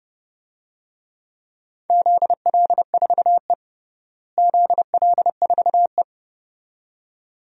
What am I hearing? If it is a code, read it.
Morse code, heard as a keyed tone, about 30 wpm, 700 Hz. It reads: ZL4E ZL4E
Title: DXCC-100 Call Signs - 4 Characters - Repeated 2x - 30wpm